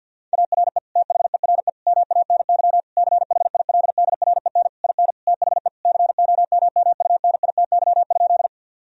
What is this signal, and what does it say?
Morse code, heard as a keyed tone, about 50 wpm, 700 Hz. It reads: ARE THERE MANY CHILDREN IN THE COMMUNITY?
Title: Sentences from Top 1000 Words - Review Part 4 - 50wpm